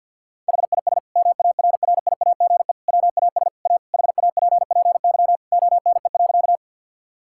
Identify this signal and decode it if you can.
Morse code, heard as a keyed tone, about 50 wpm, 700 Hz. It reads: HIS MARRIAGE WAS A HAPPY ONE.
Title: Sentences from Top 1000 Words - Review Part 5 - 50wpm